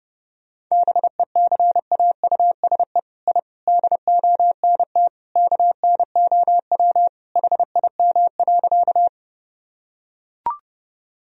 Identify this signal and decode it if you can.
Morse code, heard as a keyed tone, about 30 wpm, 700 Hz. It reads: BECAUSE I DONT KNOW HIM.
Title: Sentences from Top 1000 Words - Review Part 3 - 30wpm